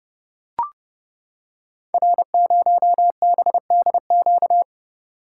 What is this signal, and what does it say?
Morse code, heard as a keyed tone, about 30 wpm, 700 Hz. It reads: R0BDQ
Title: Code Groups - 5 Letter:Numbers - 30wpm